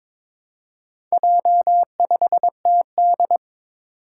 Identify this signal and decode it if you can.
Morse code, heard as a keyed tone, about 22 wpm, 700 Hz. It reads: J5TD
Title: Four Challenging Letter-Number Combinations 22wpm